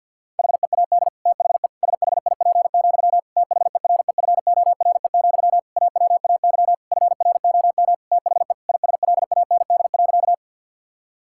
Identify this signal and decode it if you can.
Morse code, heard as a keyed tone, about 52 wpm, 700 Hz. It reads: HEAD THE SHIP, THEREFORE, AWAY FROM THE ISLAND.
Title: Sentences from Top 1000 Words - Review Part 5 - 50wpm